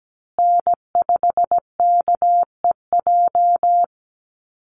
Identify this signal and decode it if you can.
Morse code, heard as a keyed tone, about 17 wpm, 700 Hz. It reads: N5KEJ
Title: USA Call Signs Repeated 2x 17wpm